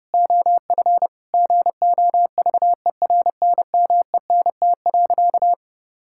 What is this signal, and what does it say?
Morse code, heard as a keyed tone, about 30 wpm, 700 Hz. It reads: OF GOVERNMENT.
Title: Sentences from Top 1000 Words - Review Part 6 - 30wpm